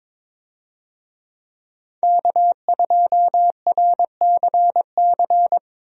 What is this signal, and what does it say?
Morse code, heard as a keyed tone, about 22 wpm, 700 Hz. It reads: K2RCC
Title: USA Call Signs - 5 Characters - Repeated 2x - 22wpm